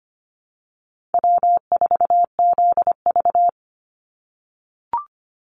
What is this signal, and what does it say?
Morse code, heard as a keyed tone, about 25 wpm, 700 Hz. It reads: W4ZV
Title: NA SOTA Chaser Callsigns 25wpm